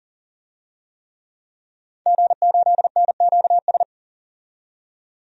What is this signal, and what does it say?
Morse code, heard as a keyed tone, about 40 wpm, 700 Hz. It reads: G8NQS
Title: DXCC-100 Call Signs - 5 Characters - 40wpm